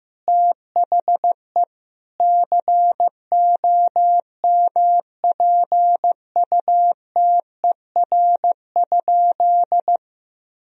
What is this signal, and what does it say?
Morse code, heard as a keyed tone, about 15 wpm, 700 Hz. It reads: THE COMPUTER?
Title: Sentences from Top 1000 Words - Review Part 3 - 15wpm